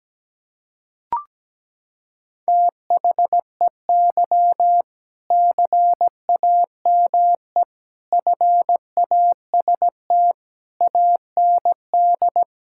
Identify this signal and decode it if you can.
Morse code, heard as a keyed tone, about 17 wpm, 700 Hz. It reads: THEY CAME FAST AND
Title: Sentences from Top 1600 Words 17wpm